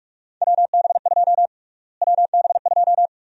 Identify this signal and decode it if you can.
Morse code, heard as a keyed tone, about 46 wpm, 700 Hz. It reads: WB2 WB2
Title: USA Call Sign Prefixes - Repeated 2x - 45wpm